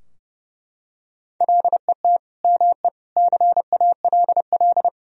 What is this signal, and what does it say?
Morse code, heard as a keyed tone, about 32 wpm, 700 Hz. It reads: LET ME CALL
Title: Sentences from Top 1000 Words - Review Part 2 - 30wpm